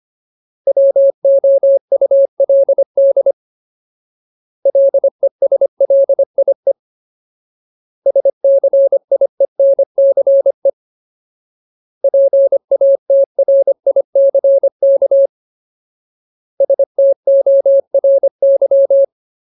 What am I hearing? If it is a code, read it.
Morse code, heard as a keyed tone, about 25 wpm, 550 Hz. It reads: WOULD LESLIE SCIENCE PATRICK STORY
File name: Ultimate Word List / 25WPM Ultimate